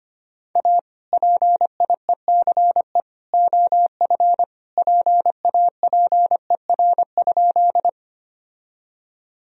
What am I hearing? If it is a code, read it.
Morse code, heard as a keyed tone, about 25 wpm, 700 Hz. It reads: A PIECE OF PAPER?
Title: Sentences from Top 1000 Words - Review Part 2 - 25wpm